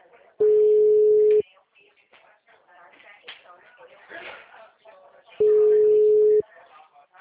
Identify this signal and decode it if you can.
Ringback tone (Europe, 425 Hz), with speech.
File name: out- -2009- - - .77720